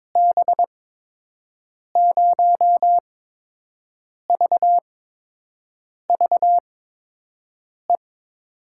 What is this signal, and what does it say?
Morse code, heard as a keyed tone, about 22 wpm, spaced as for 6 wpm, 700 Hz. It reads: B0VVE
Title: USA Call Signs - 8x Character Spacing - 22wpm